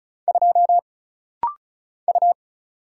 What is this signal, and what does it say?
Morse code, heard as a keyed tone, about 36 wpm, 700 Hz. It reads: 2 U
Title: Single Letter-Number - Rapid-Fire 35wpm